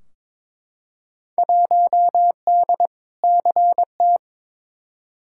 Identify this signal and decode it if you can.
Morse code, heard as a keyed tone, about 22 wpm, 700 Hz. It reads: 1D CT
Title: Field Day Exchanges - 22wpm